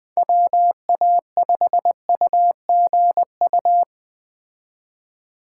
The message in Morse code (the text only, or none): WA5UGU